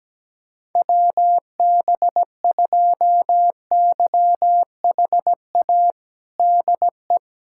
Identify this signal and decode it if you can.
Morse code, heard as a keyed tone, about 17 wpm, 700 Hz. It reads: WB2YHA DE
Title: USA Callsign DE Callsign - 6 Characters - 17wpm